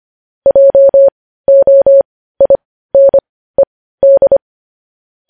Morse code, heard as a keyed tone, about 25 wpm, spaced as for 15 wpm, 550 Hz. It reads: JOINED
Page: News headlines in morse code at 15 wpm